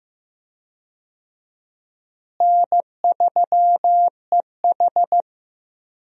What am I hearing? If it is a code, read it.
Morse code, heard as a keyed tone, about 16 wpm, 700 Hz. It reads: N3EH
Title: USA Call Signs - 4 Characters - Repeated 3x - 15wpm